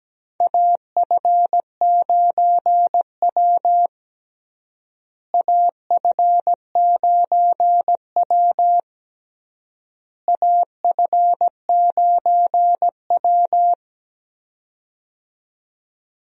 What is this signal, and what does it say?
Morse code, heard as a keyed tone, about 17 wpm, 700 Hz. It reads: AF9W AF9W AF9W
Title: USA Call Signs - 4 Characters - Repeated 3x - 17wpm